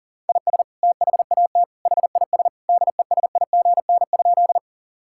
Morse code, heard as a keyed tone, about 40 wpm, 700 Hz. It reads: IS THAT HIS DESIGN?
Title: Sentences from Top 1000 Words - Review Part 5 - 40wpm